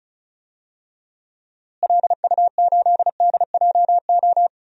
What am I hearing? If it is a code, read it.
Morse code, heard as a keyed tone, about 36 wpm, 700 Hz. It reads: LU8DJO
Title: DXCC-100 Call Signs - 6 Characters - Repeated 3x - 35wpm